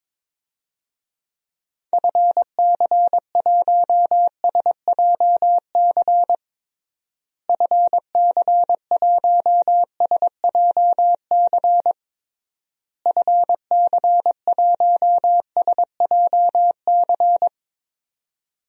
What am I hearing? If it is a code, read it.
Morse code, heard as a keyed tone, about 22 wpm, 700 Hz. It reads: FC1SJC FC1SJC FC1SJC
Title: DXCC-100 Call Signs - 6 Characters - Repeated 3x - 22wpm